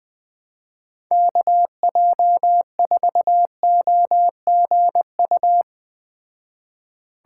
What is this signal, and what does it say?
Morse code, heard as a keyed tone, about 20 wpm, 700 Hz. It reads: KJ4OGU